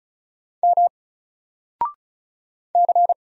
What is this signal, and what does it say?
Morse code, heard as a keyed tone, about 36 wpm, 700 Hz. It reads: M C
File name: Single-Letter-Number-35wpm